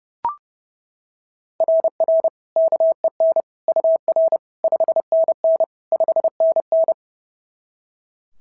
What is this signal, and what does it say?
Morse code, heard as a keyed tone, about 30 wpm, 650 Hz. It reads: RR KEN UR 5NN 5NN
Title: SOTA QSO Practice 30wpm